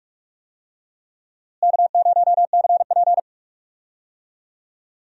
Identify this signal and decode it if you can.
Morse code, heard as a keyed tone, about 46 wpm, 700 Hz. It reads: K0CP
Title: USA Call Signs - 4 Characters - Repeated 3x - 45wpm